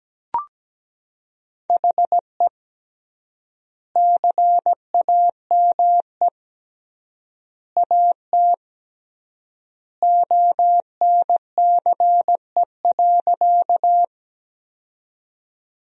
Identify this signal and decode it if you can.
Morse code, heard as a keyed tone, about 17 wpm, 700 Hz. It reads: HE CAME AT ONCE.
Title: 3x Word-Spacing - Sentences from Top 300 Words - 17wpm